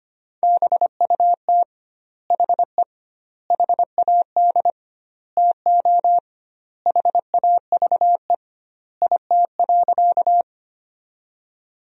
Morse code, heard as a keyed tone, about 25 wpm, 700 Hz. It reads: BUT HE HAD TO HAVE IT.
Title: 2x Word-Spacing - Sentences from Top 100 Words - 25wpm